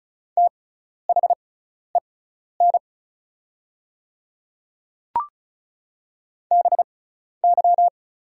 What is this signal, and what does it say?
Morse code, heard as a keyed tone, about 36 wpm, spaced as for 12 wpm, 700 Hz. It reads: THEN BY
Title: 6x Character Spacing - Top 100 Words - 35wpm